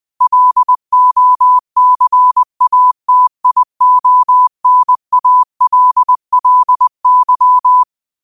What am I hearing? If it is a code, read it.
Morse code, heard as a keyed tone, about 20 wpm, 1000 Hz. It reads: LOCATIONALLY